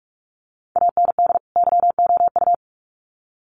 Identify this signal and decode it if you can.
Morse code, heard as a keyed tone, about 44 wpm, 700 Hz. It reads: AND YOU
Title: Sets of 2 Words 45wpm